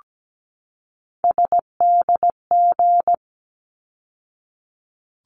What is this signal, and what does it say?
Morse code, heard as a keyed tone, about 17 wpm, 700 Hz. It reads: SDG
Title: ARRL Sections 17wpm